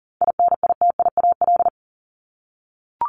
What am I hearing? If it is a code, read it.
Morse code, heard as a keyed tone, about 40 wpm, 700 Hz. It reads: INITIAL
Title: Top 1501-1750 Words 40wpm